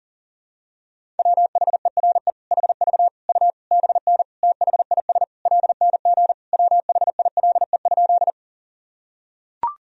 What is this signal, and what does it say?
Morse code, heard as a keyed tone, about 37 wpm, 700 Hz. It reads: WHERE HV U BN THIS LNG WHILE?
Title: Sentences from Top 300 Words and Abbreviations 40wpm